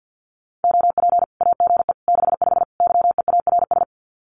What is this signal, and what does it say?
Morse code, heard as a keyed tone, about 50 wpm, 700 Hz. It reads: OP AGE 65 YEARS
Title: Common QSO Phrases 50wpm